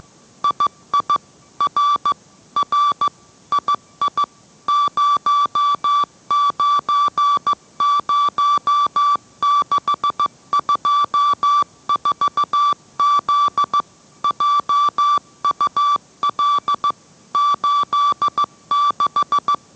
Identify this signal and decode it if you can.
Morse code, heard as a keyed tone, about 15 wpm, 1160 Hz. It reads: II R R II 090624Z JUL 86